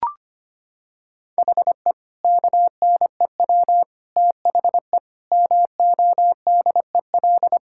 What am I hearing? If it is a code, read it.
Morse code, heard as a keyed tone, about 25 wpm, 700 Hz. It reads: HE KNEW THE MODEL
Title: Sentences from Top 1000 Words - Review Part 5 - 25wpm